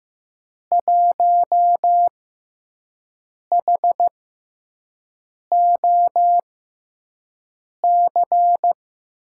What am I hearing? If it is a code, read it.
Morse code, heard as a keyed tone, about 15 wpm, spaced as for 5 wpm, 700 Hz. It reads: 1HOC
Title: USA Call Signs - 6x Character Spacing - 15wpm